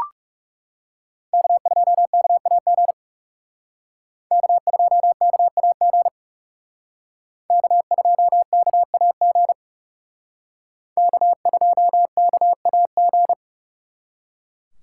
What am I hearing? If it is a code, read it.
Morse code, heard as a keyed tone, about 38 wpm, 700 Hz. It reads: K2KAG K2KAG K2KAG K2KAG